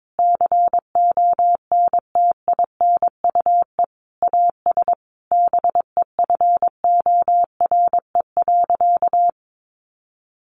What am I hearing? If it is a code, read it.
Morse code, heard as a keyed tone, about 22 wpm, 700 Hz. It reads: CONTINUE AS BEFORE.
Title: Sentences from Top 1400 Words 22wpm